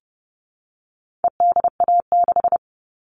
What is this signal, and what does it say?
Morse code, heard as a keyed tone, about 30 wpm, 700 Hz. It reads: EDA6
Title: All Cut Numbers - 4-Digit 30wpm